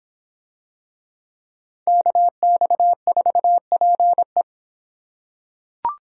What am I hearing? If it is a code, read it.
Morse code, heard as a keyed tone, about 26 wpm, 700 Hz. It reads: KX4PE